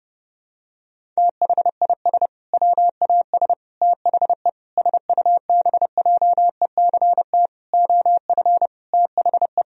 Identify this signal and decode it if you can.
Morse code, heard as a keyed tone, about 32 wpm, 700 Hz. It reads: THIS WAS THE SUBJECT OF THE
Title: Sentences from Top 1000 Words - Review Part 1 - 30wpm